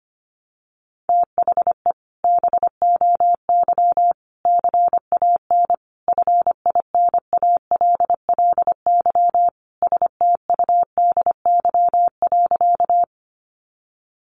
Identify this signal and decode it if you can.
Morse code, heard as a keyed tone, about 25 wpm, 700 Hz. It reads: THE BOY CAN FINALLY STUDY.
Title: Additional Easy Phrases 25wpm